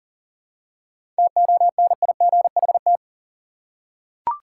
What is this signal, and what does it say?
Morse code, heard as a keyed tone, about 38 wpm, 700 Hz. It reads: TONIGHT
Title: Seven Letter Words - 40wpm